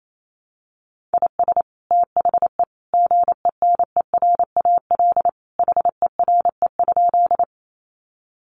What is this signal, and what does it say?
Morse code, heard as a keyed tone, about 28 wpm, 700 Hz. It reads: IS THE GENERAL HERE?
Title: Sentences from Top 600 Words 28wpm